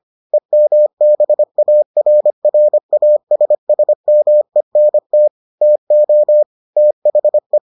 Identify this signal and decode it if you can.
Morse code, heard as a keyed tone, about 25 wpm, 600 Hz. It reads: EMBARRASSMENT TO THE